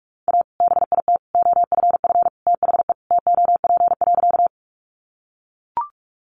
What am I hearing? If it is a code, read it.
Morse code, heard as a keyed tone, about 44 wpm, 700 Hz. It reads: A BIT OFF THE TOP.
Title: Sentences from Top 900 Words 45wpm